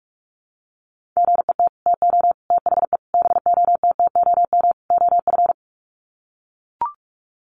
Morse code, heard as a keyed tone, about 44 wpm, 700 Hz. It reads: GET TO THE BOTTOM OF
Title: Sets of 5 Words Encore 45wpm